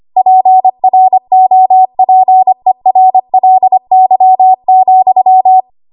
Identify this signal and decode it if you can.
Morse code, heard as a keyed tone, about 25 wpm, 750 Hz. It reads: PROPERLY,
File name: Archive / 25 WPM / 25WPM